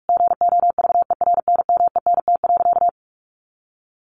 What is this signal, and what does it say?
Morse code, heard as a keyed tone, about 44 wpm, 700 Hz. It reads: GOVERNMENT.
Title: Sentences from Top 1100 Words 45wpm